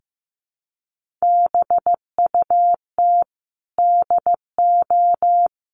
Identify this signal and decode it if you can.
Morse code, heard as a keyed tone, about 14 wpm, 700 Hz. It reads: BUT DO
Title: Sentences from Top 1500 Words 15wpm